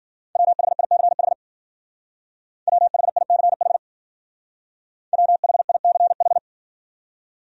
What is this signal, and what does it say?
Morse code, heard as a keyed tone, about 52 wpm, 700 Hz. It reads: WHICH WHICH WHICH